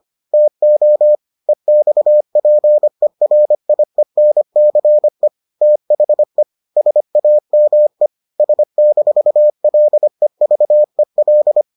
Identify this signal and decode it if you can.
Morse code, heard as a keyed tone, about 25 wpm, 600 Hz. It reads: TO EXPERIENCE THE SAME S-LEVEL